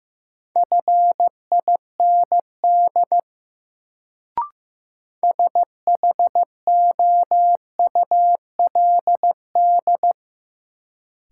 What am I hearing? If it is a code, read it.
Morse code, heard as a keyed tone, about 15 wpm, 700 Hz. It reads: FIND SHOULD
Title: Sets of 2 Words - New Word Drill - 15wpm